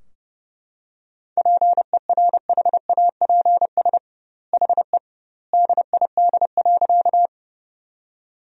Morse code, heard as a keyed tone, about 29 wpm, 700 Hz. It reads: PERHAPS HE DID.
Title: 2x Word-Spacing - Sentences from Top 500 Words - 30wpm